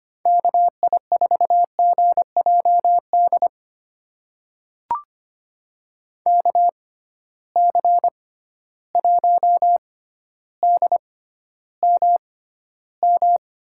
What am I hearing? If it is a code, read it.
Morse code, heard as a keyed tone, about 25 wpm, spaced as for 13 wpm, 700 Hz. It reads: KI4GJD KC1DMM
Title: USA Call Signs - 6 Characters - 6x Character Spacing - 25wpm